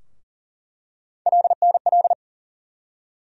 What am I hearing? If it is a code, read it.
Morse code, heard as a keyed tone, about 40 wpm, 700 Hz. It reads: LNL
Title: USA Call Sign Postfixes - 3 Letters 40wpm